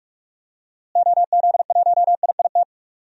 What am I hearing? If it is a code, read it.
Morse code, heard as a keyed tone, about 41 wpm, 700 Hz. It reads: OZ1IIT
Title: DXCC-100 Call Signs - 6 Characters - Repeated 3x - 45wpm